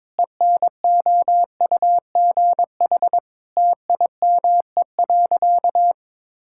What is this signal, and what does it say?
Morse code, heard as a keyed tone, about 22 wpm, 700 Hz. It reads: ENOUGH TIME.
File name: practice / 22wpm / Sentences with 4-Word Phrases - 22wpm